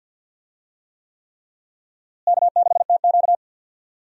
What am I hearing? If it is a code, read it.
Morse code, heard as a keyed tone, about 54 wpm, 700 Hz. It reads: K6TQ
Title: USA Call Signs - 4 Characters - Repeated 3x - 50wpm